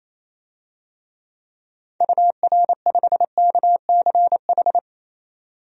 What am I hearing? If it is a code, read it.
Morse code, heard as a keyed tone, about 28 wpm, 700 Hz. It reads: UR5KCH
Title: DXCC-100 Call Signs - 6 Characters - 28wpm